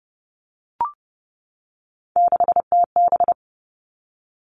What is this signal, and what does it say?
Morse code, heard as a keyed tone, about 30 wpm, 700 Hz. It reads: 6TB
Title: All Cut Numbers - 3-Digit 30wpm